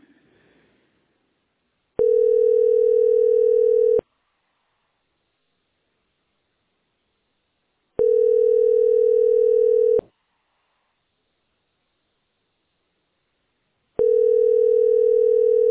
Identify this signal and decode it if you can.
Ringback tone (North America), 440 + 480 Hz.